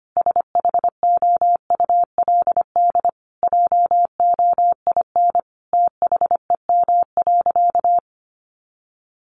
Morse code, heard as a keyed tone, about 25 wpm, 700 Hz. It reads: SHOULD JOIN THEM.
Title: Sentences from Top 1400 Words 25wpm